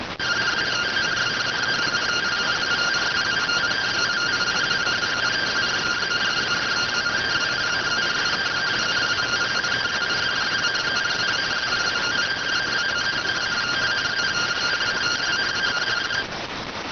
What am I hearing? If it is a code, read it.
Two-tone teleprinter signal, about 170 Hz shift.